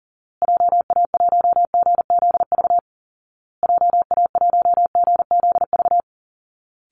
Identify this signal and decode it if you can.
Morse code, heard as a keyed tone, about 40 wpm, 700 Hz. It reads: JA1GZV JA1GZV
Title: Call Signs Repeated 2x - 40wpm